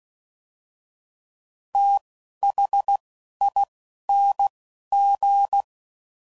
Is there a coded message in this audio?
Yes: Morse code from 1.7 to 5.6 s, heard as a keyed tone, about 16 wpm, spaced as for 11 wpm, 800 Hz. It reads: THING